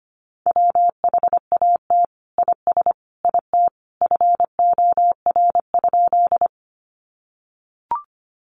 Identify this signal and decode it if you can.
Morse code, heard as a keyed tone, about 25 wpm, 700 Hz. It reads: WHAT IS IT FOR?
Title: Sentences from Top 100 Words 25wpm